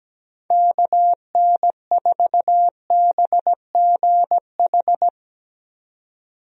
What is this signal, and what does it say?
Morse code, heard as a keyed tone, about 17 wpm, 700 Hz. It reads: KN4BGH
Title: USA Call Signs - 6 Characters - 17wpm